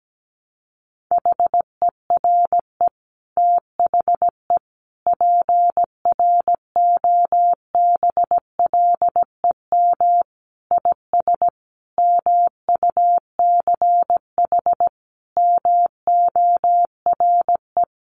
Morse code, heard as a keyed tone, about 17 wpm, 700 Hz. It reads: HERE THE PROBLEM IS MUCH MORE
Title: Sentences from Top 1200 Words 17wpm